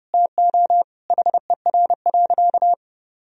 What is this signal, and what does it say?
Morse code, heard as a keyed tone, about 32 wpm, 700 Hz. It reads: TO HER.
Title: Sentences from Top 1000 Words - Review Part 4 - 30wpm